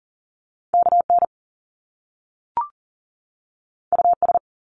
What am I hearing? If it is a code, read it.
Morse code, heard as a keyed tone, about 42 wpm, 700 Hz. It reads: KN US
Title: Top Two Letter Combinations in Words 40wpm